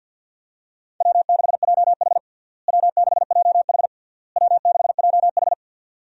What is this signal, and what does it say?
Morse code, heard as a keyed tone, about 49 wpm, 700 Hz. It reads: W6JH W6JH W6JH
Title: USA Call Signs - 4 Characters - Repeated 3x - 50wpm